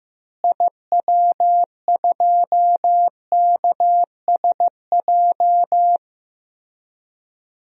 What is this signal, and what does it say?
Morse code, heard as a keyed tone, about 15 wpm, 700 Hz. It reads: IW2KSJ
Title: DXCC-100 Call Signs - 6 Characters - Repeated 2x - 15wpm